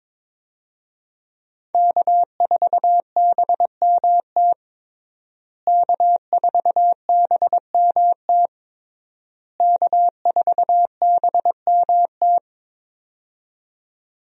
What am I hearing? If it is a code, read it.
Morse code, heard as a keyed tone, about 22 wpm, 700 Hz. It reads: K4BMT K4BMT K4BMT
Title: USA Call Signs - 5 Characters - Repeated 3x - 22wpm